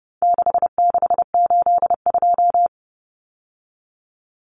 Morse code, heard as a keyed tone, about 30 wpm, 700 Hz. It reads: 6682
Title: Numbers - 4-Digit 30wpm